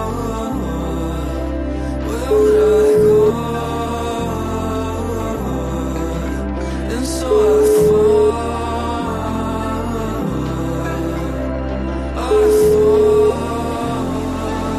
Ringback tone (Europe, 425 Hz), with noise.